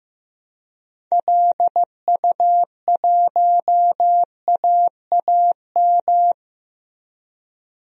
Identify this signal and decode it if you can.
Morse code, heard as a keyed tone, about 15 wpm, 700 Hz. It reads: LU1AAM
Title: DXCC-100 Call Signs - 6 Characters - 15wpm